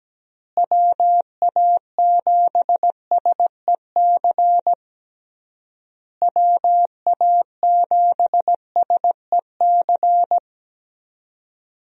Morse code, heard as a keyed tone, about 17 wpm, 700 Hz. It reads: WA7SEC WA7SEC
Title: USA Call Signs - 6 Characters - Repeated 3x - 17wpm